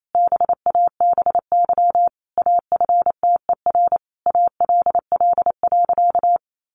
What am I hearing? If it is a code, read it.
Morse code, heard as a keyed tone, about 28 wpm, 700 Hz. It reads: BABY AFTER ALL.
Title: Sentences from Top 1300 Words 28wpm